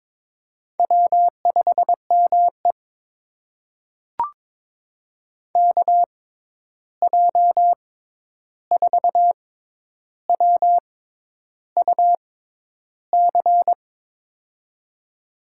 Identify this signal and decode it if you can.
Morse code, heard as a keyed tone, about 22 wpm, spaced as for 10 wpm, 700 Hz. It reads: W5ME KJ4WUC
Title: USA Call Signs - 6x Character Spacing - 22wpm